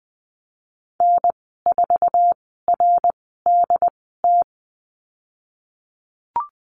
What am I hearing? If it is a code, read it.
Morse code, heard as a keyed tone, about 20 wpm, spaced as for 15 wpm, 700 Hz. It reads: N4RDT
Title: 076 Lesson 13g - Letter L - Call Signs 20wpm